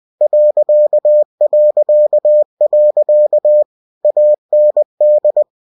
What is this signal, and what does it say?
Morse code, heard as a keyed tone, about 20 wpm, 600 Hz. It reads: ... AND
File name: cw-books / winnie-the-pooh / winnie-20wpm-page-0019